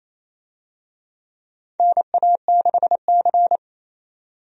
Morse code, heard as a keyed tone, about 28 wpm, 700 Hz. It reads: NA6C
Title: USA Call Signs - 4 Characters - 28wpm